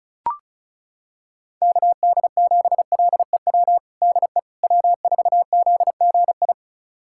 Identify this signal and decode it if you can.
Morse code, heard as a keyed tone, about 36 wpm, 700 Hz. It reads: KD7LEW DE W4ZGI
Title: USA Callsign DE Callsign 35wpm